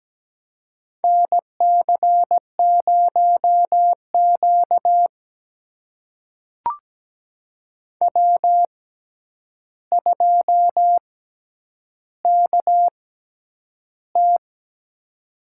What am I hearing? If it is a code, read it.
Morse code, heard as a keyed tone, about 17 wpm, spaced as for 9 wpm, 700 Hz. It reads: NC0Q W2KT
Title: USA Call Signs - 4 Characters - 6x Character Spacing - 17wpm